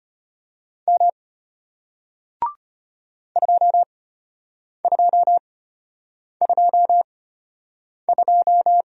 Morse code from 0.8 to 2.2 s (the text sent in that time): M